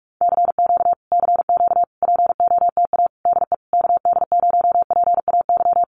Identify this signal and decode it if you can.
Morse code, heard as a keyed tone, about 44 wpm, 700 Hz. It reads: CQ CQ POTA DE KD0PAY
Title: Common QSO Phrases Encore 45wpm